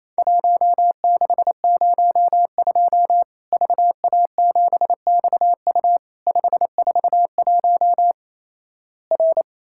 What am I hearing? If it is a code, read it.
Morse code, heard as a keyed tone, about 28 wpm, 700 Hz. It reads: 1602 VA7XU 541 R
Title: Worked All Europe DX Contest - QTC Exchange - 28wpm